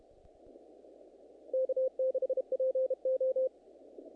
Morse code, heard as a keyed tone, about 32 wpm, 520 Hz. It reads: K6PO
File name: contests / naqp-cw-k5tr-op-k5pi- -1900